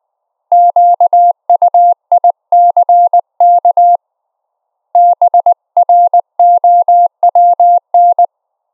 Morse code, heard as a keyed tone, about 19 wpm, 700 Hz. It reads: QUICK BROWN
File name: Op E 700 Hz